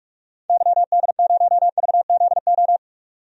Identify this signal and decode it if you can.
Morse code, heard as a keyed tone, about 46 wpm, 700 Hz. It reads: YD0VZO
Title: DXCC-100 Call Signs - 6 Characters - 45wpm